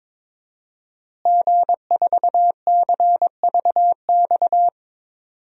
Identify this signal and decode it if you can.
Morse code, heard as a keyed tone, about 22 wpm, 700 Hz. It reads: G4CVX
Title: DXCC-100 Call Signs - 5 Characters - Repeated 2x - 22wpm